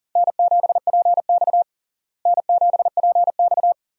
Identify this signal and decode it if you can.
Morse code, heard as a keyed tone, about 37 wpm, 700 Hz. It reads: N7PX N7PX
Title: USA Call Signs - 4 Characters - Repeated 2x - 40wpm